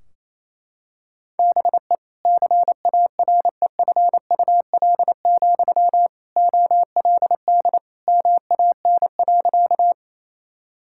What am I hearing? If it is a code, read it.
Morse code, heard as a keyed tone, about 28 wpm, 700 Hz. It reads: BE CAREFUL, OLD MAN.
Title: Sentences from Top 1600 Words 28wpm